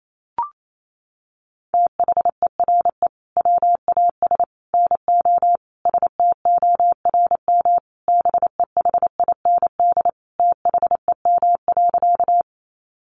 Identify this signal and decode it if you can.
Morse code, heard as a keyed tone, about 28 wpm, 700 Hz. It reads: THERE WAS NO STORM BEHIND THEM.
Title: Sentences from Top 1500 Words 28wpm